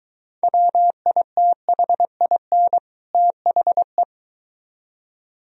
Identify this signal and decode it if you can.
Morse code, heard as a keyed tone, about 23 wpm, 700 Hz. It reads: WITHIN THE